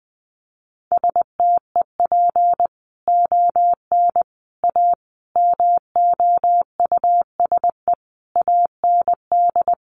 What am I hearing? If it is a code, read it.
Morse code, heard as a keyed tone, about 20 wpm, 700 Hz. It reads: STEP ON A MOUSE AND
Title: A Sound Of Thunder 20wpm